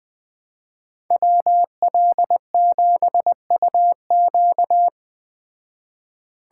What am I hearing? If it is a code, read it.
Morse code, heard as a keyed tone, about 20 wpm, 700 Hz. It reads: WL7UQ
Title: Speed Racing - USA 2x2 Call Signs 17wpm